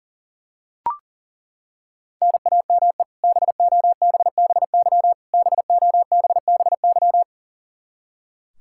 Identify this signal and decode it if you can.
Morse code, heard as a keyed tone, about 40 wpm, 700 Hz. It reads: NAME BOBBY BOBBY
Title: Practice QSO - Exchange 1 - Beginning - 40wpm